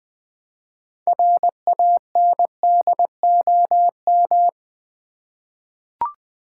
Morse code, heard as a keyed tone, about 20 wpm, 700 Hz. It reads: RANDOM
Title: 6 Letter Words - First 3 Letters Given - 20wpm